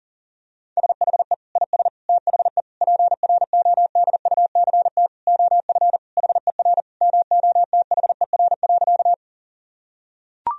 Morse code, heard as a keyed tone, about 40 wpm, 700 Hz. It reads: SHE IS THE PRODUCT OF HER MOTHER.
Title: Sentences from Top 1000 Words - Review Part 2 - 40wpm